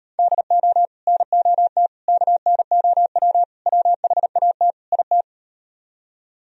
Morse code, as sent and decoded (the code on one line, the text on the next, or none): -.. --- / -. --- - / -.- -. --- .-- / .-- .... .- - / .. -
DO NOT KNOW WHAT IT